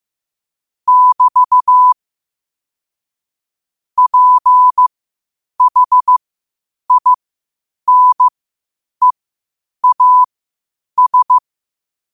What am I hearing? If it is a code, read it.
Morse code, heard as a keyed tone, about 15 wpm, spaced as for 8 wpm, 1000 Hz. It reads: <BT> PHINEAS
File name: qotdfiles / 8wpm